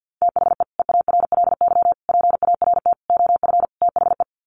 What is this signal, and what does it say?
Morse code, heard as a keyed tone, about 50 wpm, 700 Hz. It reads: THE EARLY PART OF THE
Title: Sets of 5 Words Encore2 50wpm